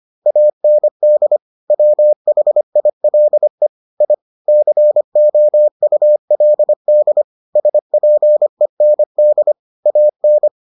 Morse code, heard as a keyed tone, about 25 wpm, 600 Hz. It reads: AND WHILE I COULD SPEND AN